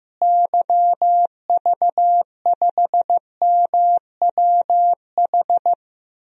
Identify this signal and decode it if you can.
Morse code, heard as a keyed tone, about 15 wpm, 700 Hz. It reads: YV5MWH
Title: DXCC-100 Call Signs - 6 Characters - Repeated 2x - 15wpm